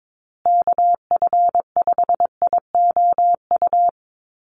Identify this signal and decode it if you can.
Morse code, heard as a keyed tone, about 22 wpm, 700 Hz. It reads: KF5IOU
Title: USA Call Signs Repeated 2x 22wpm